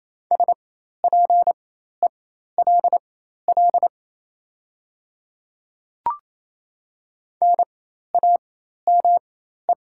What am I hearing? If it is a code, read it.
Morse code, heard as a keyed tone, about 28 wpm, spaced as for 13 wpm, 700 Hz. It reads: SPELL NAME
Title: 4x Character Spacing - Top 500 Words - 28wpm